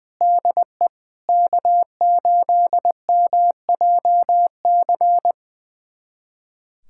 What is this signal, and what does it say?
Morse code, heard as a keyed tone, about 20 wpm, 700 Hz. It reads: DE K8MJC
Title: USA Callsign DE Callsign 20wpm